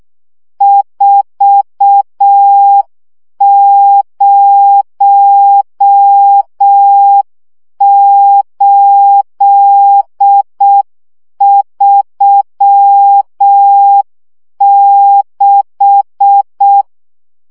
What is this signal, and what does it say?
Morse code, heard as a keyed tone, about 6 wpm, 800 Hz. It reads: 40836
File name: morze vizsga 2011-3 5